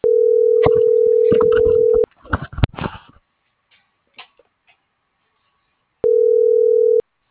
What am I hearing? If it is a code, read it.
Ringback tone (North America), 440 + 480 Hz.
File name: CENTRAL SOTOVA / out carlos.sotova